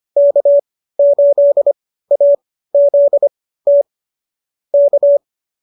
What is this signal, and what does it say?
Morse code, heard as a keyed tone, about 25 wpm, spaced as for 15 wpm, 570 Hz. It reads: K8AZT K